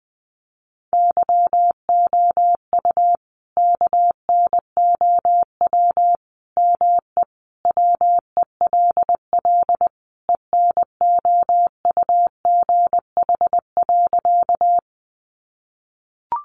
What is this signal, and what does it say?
Morse code, heard as a keyed tone, about 20 wpm, 700 Hz. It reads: YOU KNOW ME WELL ENOUGH.
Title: Sentences from Top 400 Words 20wpm